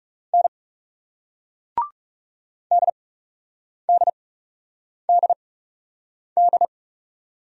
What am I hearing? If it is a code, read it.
Morse code, heard as a keyed tone, about 41 wpm, spaced as for 8 wpm, 700 Hz. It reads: N DDDD